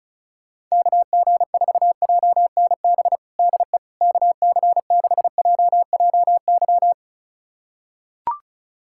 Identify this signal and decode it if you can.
Morse code, heard as a keyed tone, about 34 wpm, 700 Hz. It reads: KG4JNB DE KC6JJY
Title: USA Callsign DE Callsign 35wpm